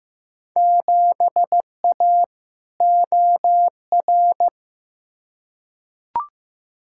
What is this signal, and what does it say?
Morse code, heard as a keyed tone, about 15 wpm, 700 Hz. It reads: 7A OR
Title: Field Day Exchanges Encore - 15wpm